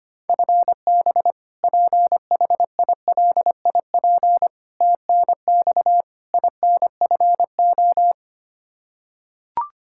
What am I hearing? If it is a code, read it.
Morse code, heard as a keyed tone, about 25 wpm, 700 Hz. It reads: FB PHILIP TNX INFO
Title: Practice QSO - Exchange 3 - More Info - 25wpm